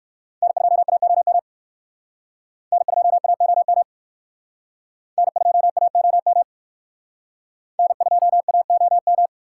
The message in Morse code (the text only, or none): N2AOM N2AOM N2AOM N2AOM